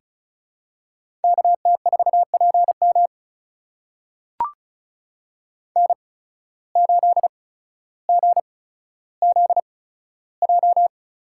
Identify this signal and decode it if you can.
Morse code, heard as a keyed tone, about 36 wpm, spaced as for 15 wpm, 700 Hz. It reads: KT4PM N8GZJ
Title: USA Call Signs - 5 Characters - 8x Character Spacing - 35wpm